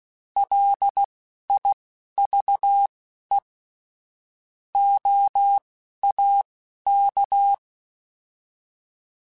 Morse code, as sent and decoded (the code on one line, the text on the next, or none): .-.. .. ...- . / --- .- -.-
LIVE OAK